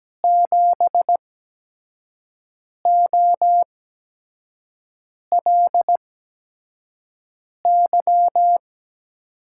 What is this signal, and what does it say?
Morse code, heard as a keyed tone, about 17 wpm, spaced as for 5 wpm, 700 Hz. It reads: 7OLY
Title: USA Call Signs - 8x Character Spacing - 17wpm